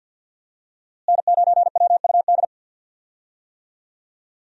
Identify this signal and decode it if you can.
Morse code, heard as a keyed tone, about 49 wpm, 700 Hz. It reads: N9WUD
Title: USA Call Signs - 5 Characters - Repeated 3x - 50wpm